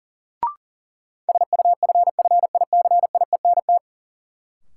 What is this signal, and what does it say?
Morse code, heard as a keyed tone, about 40 wpm, 700 Hz. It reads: SUFFICIENT
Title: Sentences from Top 1400 Words - New Word Drill - 40wpm